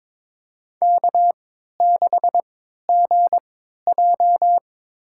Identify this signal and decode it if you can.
Morse code, heard as a keyed tone, about 21 wpm, spaced as for 12 wpm, 700 Hz. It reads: K6GJ
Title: USA Call Signs - 4 Characters - 3x Character Spacing - 22wpm